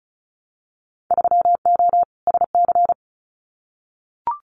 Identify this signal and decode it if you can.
Morse code, heard as a keyed tone, about 36 wpm, 700 Hz. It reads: 3O SC
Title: Winter Field Day Exchanges Encore 35wpm